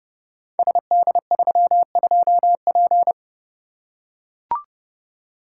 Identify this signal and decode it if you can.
Morse code, heard as a keyed tone, about 30 wpm, 700 Hz. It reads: SD32P
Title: Code Groups - 5 Letter:Numbers - 30wpm